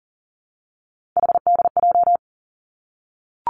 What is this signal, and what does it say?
Morse code, heard as a keyed tone, about 43 wpm, 700 Hz. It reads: HDJ
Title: Three Challenging Letter-Number Combinations 40wpm